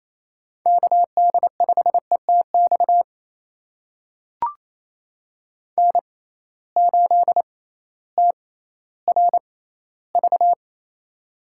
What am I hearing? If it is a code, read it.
Morse code, heard as a keyed tone, about 28 wpm, spaced as for 15 wpm, 700 Hz. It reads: KD5ETX N8TRV
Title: USA Call Signs - 6x Character Spacing - 28wpm